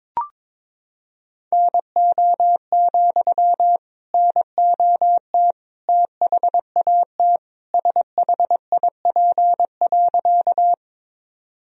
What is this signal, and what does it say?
Morse code, heard as a keyed tone, about 22 wpm, 700 Hz. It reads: NO, NOT THAT SHIP.
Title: Sentences from Top 1000 Words - Review Part 2 - 22wpm